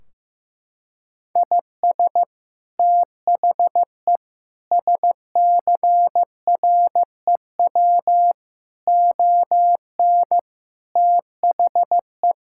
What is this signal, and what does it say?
Morse code, heard as a keyed tone, about 15 wpm, 700 Hz. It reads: IS THE SCREW ON THE
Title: Sentences from Top 1600 Words 15wpm